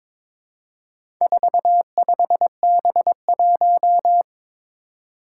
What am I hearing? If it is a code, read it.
Morse code, heard as a keyed tone, about 22 wpm, 700 Hz. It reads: 45B1
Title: Four Challenging Letter-Number Combinations 22wpm